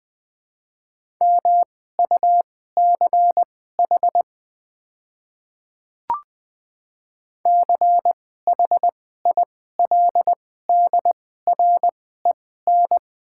Morse code, heard as a keyed tone, about 20 wpm, spaced as for 14 wpm, 700 Hz. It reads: MUCH CHILDREN
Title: 2x Character Spacing - Top 300 Words - 20wpm